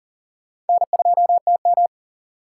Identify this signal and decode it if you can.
Morse code, heard as a keyed tone, about 56 wpm, spaced as for 49 wpm, 700 Hz. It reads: N2TM
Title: Speed Racing - USA 1x2 Call Signs 40wpm